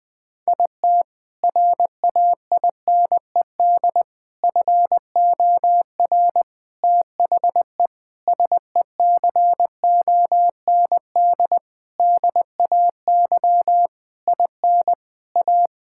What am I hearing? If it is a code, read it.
Morse code, heard as a keyed tone, about 20 wpm, 700 Hz. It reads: IT RAINED FOR THE SECOND DAY IN A
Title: Sentences with 7-Word Phrases - 20wpm